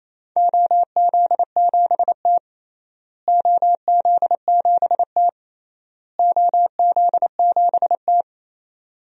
Morse code, heard as a keyed tone, about 28 wpm, 700 Hz. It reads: OZ7T OZ7T OZ7T
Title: DXCC-100 Call Signs - 4 Characters - Repeated 3x - 28wpm